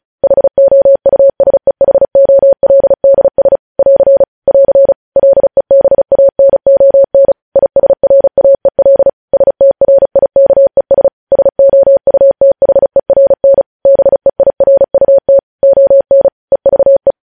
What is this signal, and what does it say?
Morse code, heard as a keyed tone, about 36 wpm, 550 Hz. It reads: HOUSEHOLDS <AR> <AR> LEBANON ISRAEL STRIKES SOUTHERN BEIRUT ON EVE
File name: podcast / News headlines-1x-35x35